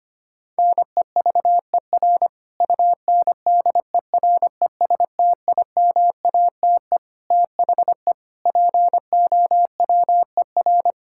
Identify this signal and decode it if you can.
Morse code, heard as a keyed tone, about 25 wpm, 700 Hz. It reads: NEVER UNDERESTIMATE THE POWER
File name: practice / 25wpm / Sentences with 4-Word Phrases - 25wpm